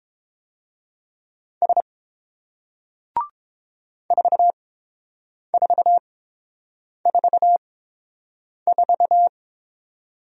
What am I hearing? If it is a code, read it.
Morse code, heard as a keyed tone, about 30 wpm, spaced as for 7 wpm, 700 Hz. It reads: S 4444